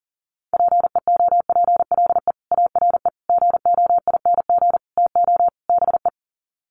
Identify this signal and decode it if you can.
Morse code, heard as a keyed tone, about 40 wpm, 700 Hz. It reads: PEOPLE ARE GOING TO BE
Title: Sets of 5 Words Encore 40wpm